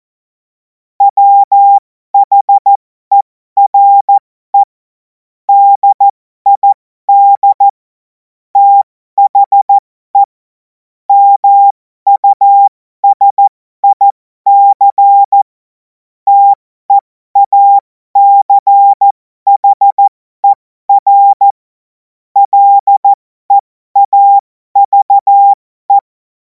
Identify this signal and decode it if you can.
Morse code, heard as a keyed tone, about 14 wpm, spaced as for 12 wpm, 800 Hz. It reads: WHERE DID THE MUSIC TEACHER LEAVE
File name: Morsekurs / Witze